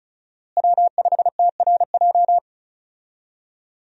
Morse code, heard as a keyed tone, about 36 wpm, 700 Hz. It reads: W5TRJ
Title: USA Call Signs - 5 Characters - Repeated 3x - 35wpm